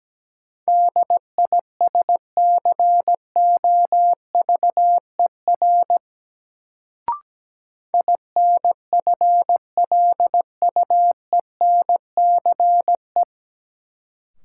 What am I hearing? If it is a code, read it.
Morse code, heard as a keyed tone, about 17 wpm, 700 Hz. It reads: DISCOVER INFLUENCE
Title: Sentences from Top 1100 Words - New Word Drill - 17wpm